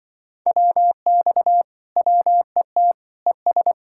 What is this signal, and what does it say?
Morse code, heard as a keyed tone, about 25 wpm, 700 Hz. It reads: WX WET ES
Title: Speed-Racing Common QSO Phrases - 20wpm